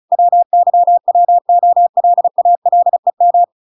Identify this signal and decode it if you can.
Morse code, heard as a keyed tone, about 34 wpm, 700 Hz. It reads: WYWOLALEM